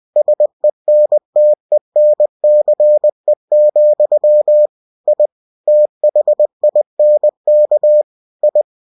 Morse code, heard as a keyed tone, about 20 wpm, 600 Hz. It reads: SENTENCE, I THINK I
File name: cw-books / winnie-the-pooh / winnie-20wpm-page-0101